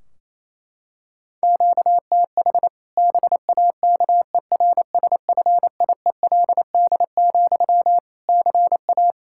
Morse code, heard as a keyed tone, about 28 wpm, 700 Hz. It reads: QTH BAKERSFIELD, CA
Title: Practice QSO - Exchange 1 - Beginning - 28wpm